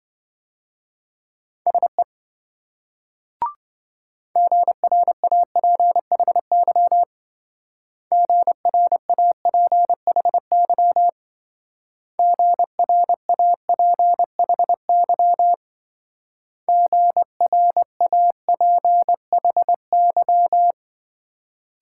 Morse code, heard as a keyed tone, about 26 wpm, 700 Hz. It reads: SE GRAPHY GRAPHY GRAPHY GRAPHY